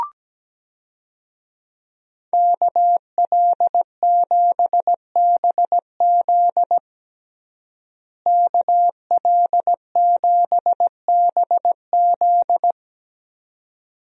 Morse code, heard as a keyed tone, about 17 wpm, 700 Hz. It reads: KL7BZ KL7BZ
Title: DXCC-100 Call Signs - 5 Characters - Repeated 2x - 17wpm